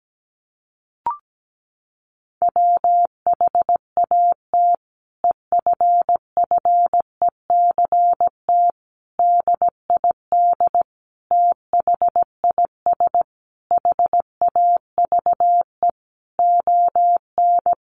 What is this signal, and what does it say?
Morse code, heard as a keyed tone, about 17 wpm, 700 Hz. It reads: WHAT EFFECT DID THIS HAVE ON
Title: Sentences from Top 1000 Words 17wpm